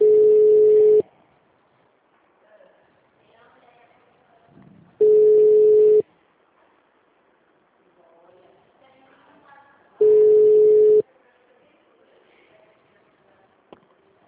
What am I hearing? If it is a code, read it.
Ringback tone (Europe, 425 Hz), with speech.
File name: out- -2015- - - .8903